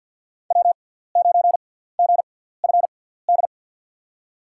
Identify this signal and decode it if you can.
Morse code, heard as a keyed tone, about 50 wpm, spaced as for 17 wpm, 700 Hz. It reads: W9GFD
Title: USA Call Signs - 6x Character Spacing - 50wpm